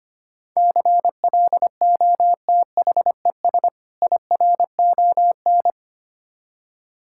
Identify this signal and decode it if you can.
Morse code, heard as a keyed tone, about 25 wpm, 700 Hz. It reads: CLOTHES IRON
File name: practice / 25wpm / Difficult Words - Speed-Racing with Hints - 25wpm